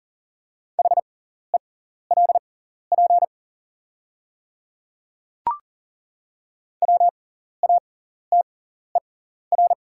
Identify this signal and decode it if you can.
Morse code, heard as a keyed tone, about 40 wpm, spaced as for 14 wpm, 700 Hz. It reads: HELP WATER
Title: 6x Character Spacing - Top 200 Words - 40wpm